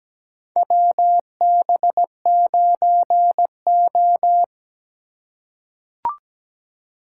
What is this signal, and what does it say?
Morse code, heard as a keyed tone, about 17 wpm, 700 Hz. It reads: WB9O
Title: USA Call Signs - 4 Characters - 3x Character Spacing - 17wpm